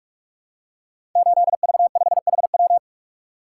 Morse code, heard as a keyed tone, about 44 wpm, 690 Hz. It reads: 8V5HW
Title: Code Groups - 5 Letter:Numbers - 45wpm